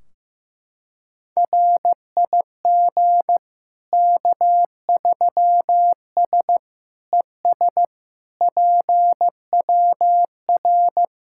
Morse code, heard as a keyed tone, about 15 wpm, 700 Hz. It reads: RIG K3S ES PWR
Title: Practice QSO - Exchange 2 - Info - 15wpm